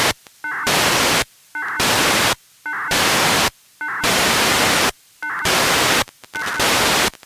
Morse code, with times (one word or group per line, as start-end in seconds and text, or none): none